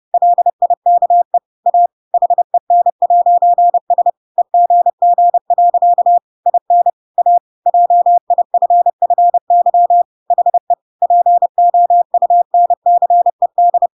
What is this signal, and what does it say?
Morse code, heard as a keyed tone, about 30 wpm, 700 Hz. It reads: LIKE A HEN'S EGG. IN A JIFFY HE POUNCED
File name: Pinocchio / Pinocchio ch5 30 30 ik1hge